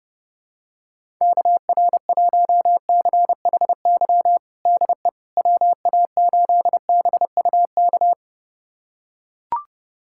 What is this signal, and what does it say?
Morse code, heard as a keyed tone, about 29 wpm, 700 Hz. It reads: KR1CHY DE WA8BUK
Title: USA Callsign DE Callsign - 6 Characters - 30wpm